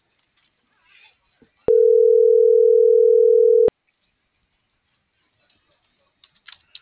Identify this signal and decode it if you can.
Ringback tone (North America), 440 + 480 Hz.